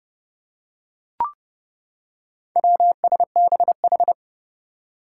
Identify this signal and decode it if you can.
Morse code, heard as a keyed tone, about 30 wpm, 700 Hz. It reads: WSBH
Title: Four Challenging Letter-Number Combinations 30wpm